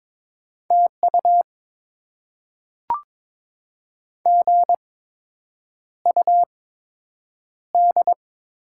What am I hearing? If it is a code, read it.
Morse code, heard as a keyed tone, about 22 wpm, spaced as for 8 wpm, 700 Hz. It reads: TU GUD
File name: Top 20 QSO Elements - 8x Character Spacing - 22wpm